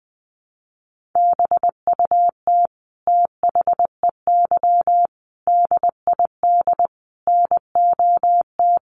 Morse code, heard as a keyed tone, about 20 wpm, 700 Hz. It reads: BUT THEY DID NOT
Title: Sentences from Top 1300 Words 20wpm